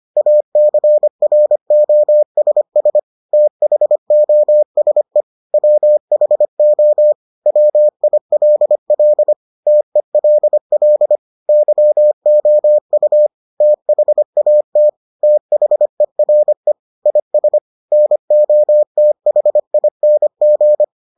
Morse code, heard as a keyed tone, about 25 wpm, 600 Hz. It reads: ACROSS THOSE WHO WILL TELL YOU THAT THERE IS NOTHING